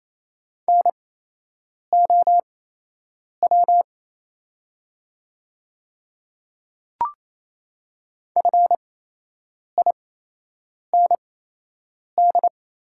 Morse code, heard as a keyed tone, about 28 wpm, spaced as for 8 wpm, 700 Hz. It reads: NOW FIND
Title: 8x Character Spacing - Top 200 Words - 28wpm